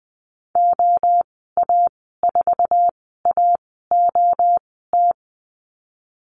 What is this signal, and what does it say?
Morse code, heard as a keyed tone, about 20 wpm, spaced as for 14 wpm, 700 Hz. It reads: OA4AOT
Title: 055 Lesson 10g - Letter r - Call Signs 20wpm